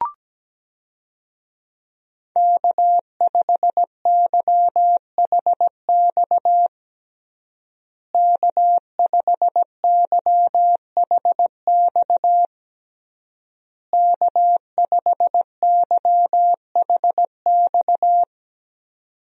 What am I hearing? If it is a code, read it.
Morse code, heard as a keyed tone, about 17 wpm, 700 Hz. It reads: K5YHX K5YHX K5YHX
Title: USA Call Signs - 5 Characters - Repeated 3x - 17wpm